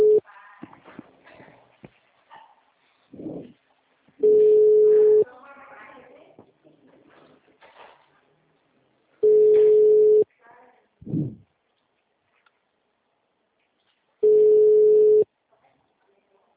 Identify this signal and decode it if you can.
Ringback tone (Europe, 425 Hz), with speech.
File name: out- -2009- - - .16176